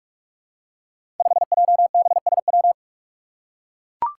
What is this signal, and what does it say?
Morse code, heard as a keyed tone, about 44 wpm, 700 Hz. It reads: 5JBSW
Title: Five Challenging Letter-Number Combinations 45wpm